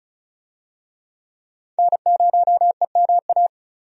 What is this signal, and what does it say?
Morse code, heard as a keyed tone, about 36 wpm, 700 Hz. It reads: N0EMA
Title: USA Call Signs - 5 Characters - 35wpm